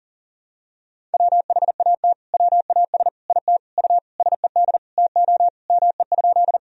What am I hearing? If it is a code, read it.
Morse code, heard as a keyed tone, about 40 wpm, 700 Hz. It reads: WHAT WAS IT U SED TO ME?
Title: Sentences from Top 200 Words and Abbreviations 40wpm